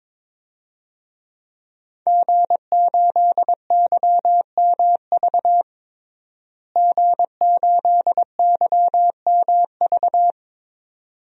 Morse code, heard as a keyed tone, about 22 wpm, 700 Hz. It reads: G8YMV G8YMV
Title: DXCC-100 Call Signs - 5 Characters - Repeated 2x - 22wpm